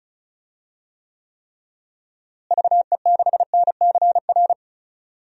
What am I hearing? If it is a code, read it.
Morse code, heard as a keyed tone, about 34 wpm, 700 Hz. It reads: VE6NCR
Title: DXCC-100 Call Signs - 6 Characters - Repeated 3x - 35wpm